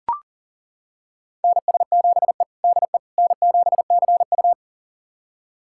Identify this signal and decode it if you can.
Morse code, heard as a keyed tone, about 40 wpm, 700 Hz. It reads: NS7E DE N7CU
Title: USA Callsign DE Callsign - 4 Characters - 40wpm